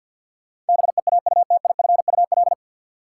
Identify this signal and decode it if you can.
Morse code, heard as a keyed tone, about 50 wpm, 700 Hz. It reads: BEAUTIFUL
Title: Sentences from Top 900 Words - New Word Drill - 50wpm